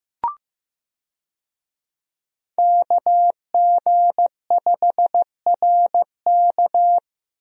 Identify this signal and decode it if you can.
Morse code, heard as a keyed tone, about 15 wpm, 700 Hz. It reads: KG5RK
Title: USA Call Signs - 5 Characters - 15wpm